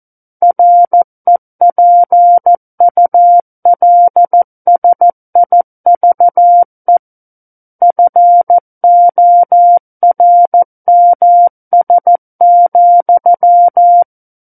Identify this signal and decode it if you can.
Morse code, heard as a keyed tone, about 14 wpm, 700 Hz. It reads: REPULSIVE FORMS,
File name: Book One / 05-Chapter 5 - 14 WPM